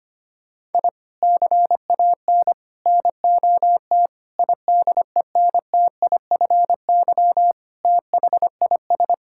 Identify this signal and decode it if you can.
Morse code, heard as a keyed tone, about 25 wpm, 700 Hz. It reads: I CAN NOT IDENTIFY THIS
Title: Sentences from Top 1000 Words - Review Part 6 - 25wpm